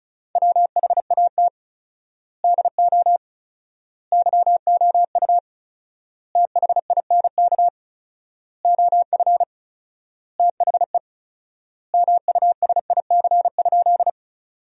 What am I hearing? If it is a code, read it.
Morse code, heard as a keyed tone, about 34 wpm, 700 Hz. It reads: WHAT DO YOU THINK OF THE MUSIC?
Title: 4x Word-Spacing - Sentences from Top 300 Words - 35wpm